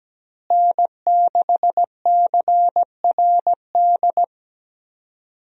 Morse code, heard as a keyed tone, about 17 wpm, 700 Hz. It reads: N6CRD
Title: USA Call Signs - 5 Characters - Repeated 3x - 17wpm